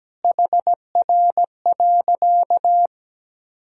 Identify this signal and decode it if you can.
Morse code, heard as a keyed tone, about 17 wpm, 700 Hz. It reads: HR.
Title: Sentences from Top 400 Words and Abbreviations 17wpm